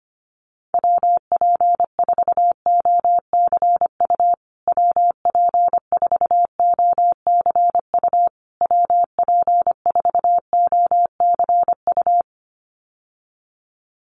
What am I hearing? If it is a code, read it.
Morse code, heard as a keyed tone, about 25 wpm, 700 Hz. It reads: WP4OCU WP4OCU WP4OCU
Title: USA Call Signs Repeated 3x 25wpm